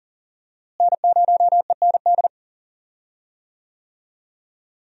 Morse code, heard as a keyed tone, about 40 wpm, 700 Hz. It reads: N0END
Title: USA Call Signs - 5 Characters - Repeated 3x - 40wpm